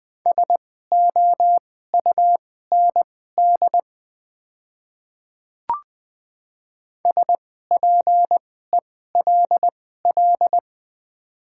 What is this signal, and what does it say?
Morse code, heard as a keyed tone, about 20 wpm, spaced as for 14 wpm, 690 Hz. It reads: SOUND SPELL
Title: 2x Character Spacing - Top 500 Words - 20wpm